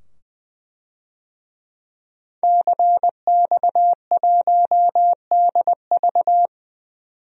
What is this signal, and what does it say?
Morse code, heard as a keyed tone, about 21 wpm, 700 Hz. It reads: CX1DV
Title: DXCC-100 Call Signs - 5 Characters - 20wpm